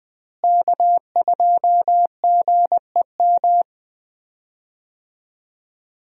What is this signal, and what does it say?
Morse code, heard as a keyed tone, about 20 wpm, 700 Hz. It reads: K2GEM
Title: USA Call Signs - 5 Characters - Repeated 2x - 20wpm